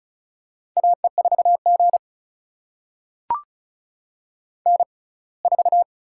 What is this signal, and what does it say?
Morse code, heard as a keyed tone, about 36 wpm, spaced as for 24 wpm, 700 Hz. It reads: AE4G N4
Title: USA Call Signs - 4 Characters - 6x Character Spacing - 35wpm